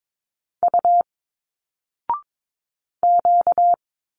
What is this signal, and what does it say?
Morse code, heard as a keyed tone, about 22 wpm, 700 Hz. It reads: U Q
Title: Single Letters 22wpm